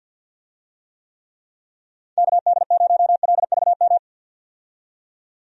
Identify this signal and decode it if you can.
Morse code, heard as a keyed tone, about 50 wpm, 700 Hz. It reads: KD0LVM
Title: USA Call Signs - 6 Characters - 50wpm